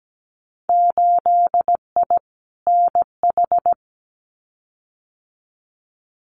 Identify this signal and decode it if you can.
Morse code, heard as a keyed tone, about 17 wpm, 700 Hz. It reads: 8I NH
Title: Winter Field Day Exchanges Encore 17wpm